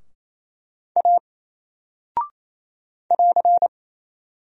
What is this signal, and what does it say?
Morse code, heard as a keyed tone, about 27 wpm, 700 Hz. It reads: A <AR>
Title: Single Character (Letter:Number:Punctuation:Prosign) 28wpm